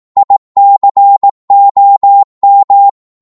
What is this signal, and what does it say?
Morse code, heard as a keyed tone, about 18 wpm, 800 Hz. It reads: ICOM
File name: Morse / qso-18wpm